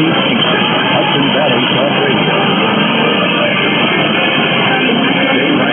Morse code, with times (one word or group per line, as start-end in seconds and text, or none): none